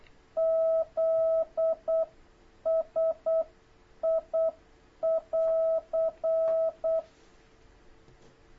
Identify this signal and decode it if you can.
Morse code, heard as a keyed tone, about 8 wpm, 660 Hz. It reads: Z S I <AR>
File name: jh8cbh / 2026cw